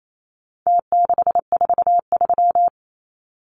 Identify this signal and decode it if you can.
Morse code, heard as a keyed tone, about 28 wpm, 700 Hz. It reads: T643
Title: Numbers - Contest Style - 4-Digit 28wpm